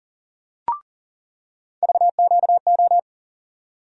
Morse code, heard as a keyed tone, about 40 wpm, 700 Hz. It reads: VQO
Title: USA Call Sign Postfixes - 3 Letters 40wpm